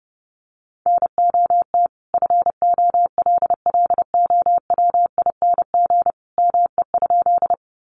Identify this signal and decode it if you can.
Morse code, heard as a keyed tone, about 28 wpm, 700 Hz. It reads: NOT FOLLOWING ME?
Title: Sentences from Top 700 Words 30wpm